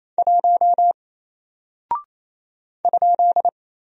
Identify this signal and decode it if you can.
Morse code, heard as a keyed tone, about 27 wpm, 700 Hz. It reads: 1 ?
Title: Single Character (Letter:Number:Punctuation:Prosign) 28wpm